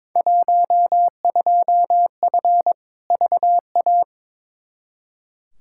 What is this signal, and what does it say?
Morse code, heard as a keyed tone, about 22 wpm, 700 Hz. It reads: 12F VA
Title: Field Day Exchanges Encore - 22wpm